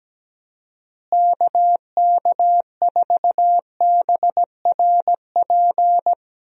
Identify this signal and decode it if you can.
Morse code, heard as a keyed tone, about 17 wpm, 700 Hz. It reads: KK4BRP
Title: USA Call Signs - 6 Characters - Repeated 2x - 17wpm